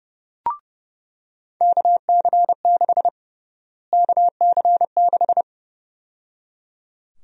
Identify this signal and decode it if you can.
Morse code, heard as a keyed tone, about 30 wpm, 700 Hz. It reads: KC6 KC6
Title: USA Call Sign Prefixes - Repeated 2x - 30wpm